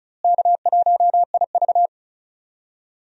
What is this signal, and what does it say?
Morse code, heard as a keyed tone, about 36 wpm, 700 Hz. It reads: K1IV
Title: USA Call Signs - 4 Characters - Repeated 3x - 35wpm